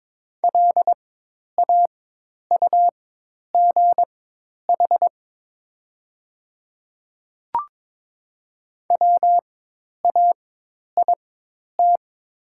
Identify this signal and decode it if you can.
Morse code, heard as a keyed tone, about 22 wpm, spaced as for 10 wpm, 700 Hz. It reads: LAUGH WAIT
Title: 4x Character Spacing - Top 500 Words - 22wpm